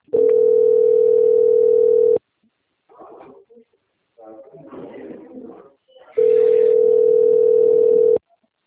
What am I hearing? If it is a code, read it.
Ringback tone (North America), 440 + 480 Hz.